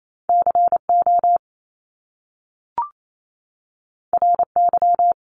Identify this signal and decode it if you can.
Morse code, heard as a keyed tone, about 28 wpm, 700 Hz. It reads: CO RY
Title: Top Two Letter Combinations in Words 28wpm